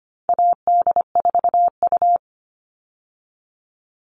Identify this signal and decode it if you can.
Morse code, heard as a keyed tone, about 25 wpm, 700 Hz. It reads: AD4U
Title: All Cut Numbers - 4-Digit 25wpm